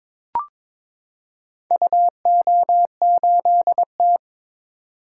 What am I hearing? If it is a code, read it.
Morse code, heard as a keyed tone, about 22 wpm, 700 Hz. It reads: UO8T
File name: practice / 22wpm / Four Letter-Number Combinations - 22wpm